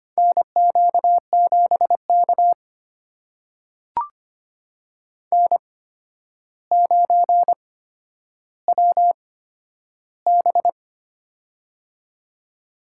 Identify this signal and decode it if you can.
Morse code, heard as a keyed tone, about 25 wpm, spaced as for 11 wpm, 700 Hz. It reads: NQ7K N9WB
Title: USA Call Signs - 4 Characters - 8x Character Spacing - 25wpm